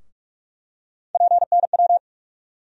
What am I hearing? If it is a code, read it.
Morse code, heard as a keyed tone, about 44 wpm, 700 Hz. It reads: PNW
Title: USA Call Sign Postfixes - 3 Letters 45wpm